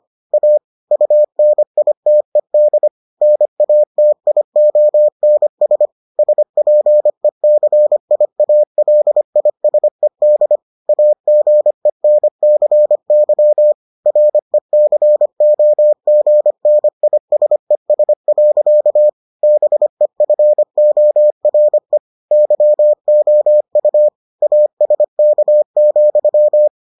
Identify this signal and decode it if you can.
Morse code, heard as a keyed tone, about 25 wpm, 600 Hz. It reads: A UNITED NATIONS SPECIALISED AGENCY RECOGNISES. BEFORE YOU ASK,